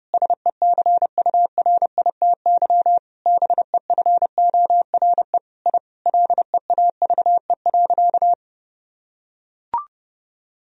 Morse code, heard as a keyed tone, about 30 wpm, 700 Hz. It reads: SECURITY BEFORE I LEAVE.
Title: Sentences from Top 1000 Words - Review Part 4 - 30wpm